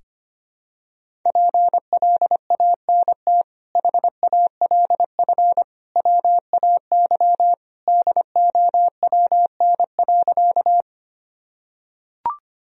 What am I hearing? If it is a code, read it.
Morse code, heard as a keyed tone, about 25 wpm, 700 Hz. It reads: PLANT HALF WAY DOWN.
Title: Sentences from Top 1000 Words - Review Part 1 - 25wpm